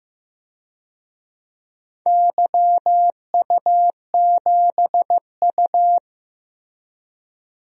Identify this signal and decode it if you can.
Morse code, heard as a keyed tone, about 15 wpm, 700 Hz. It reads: YU7U
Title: DXCC-100 Call Signs - 4 Characters - Repeated 2x - 15wpm